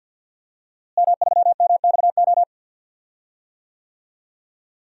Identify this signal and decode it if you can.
Morse code, heard as a keyed tone, about 46 wpm, 700 Hz. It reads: M3MXO
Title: DXCC-100 Call Signs - 5 Characters - Repeated 2x - 50wpm